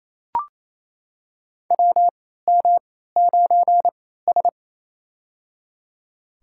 Morse code, heard as a keyed tone, about 28 wpm, spaced as for 16 wpm, 700 Hz. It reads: WM9S
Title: USA Call Signs - 4 Characters - 3x Character Spacing - 28wpm